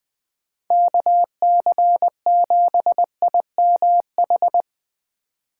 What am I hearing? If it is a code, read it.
Morse code, heard as a keyed tone, about 20 wpm, 700 Hz. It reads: KC7IMH
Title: Speed Racing - USA Call Signs 17wpm